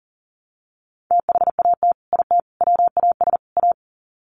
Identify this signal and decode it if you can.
Morse code, heard as a keyed tone, about 40 wpm, 700 Hz. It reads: THAT IT WAS A
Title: Sets of 4 Words 40wpm